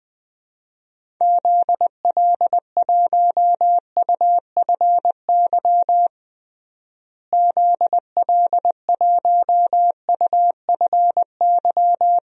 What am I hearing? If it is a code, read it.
Morse code, heard as a keyed tone, about 20 wpm, 700 Hz. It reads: ZL1UFY ZL1UFY
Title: DXCC-100 Call Signs - 6 Characters - Repeated 2x - 20wpm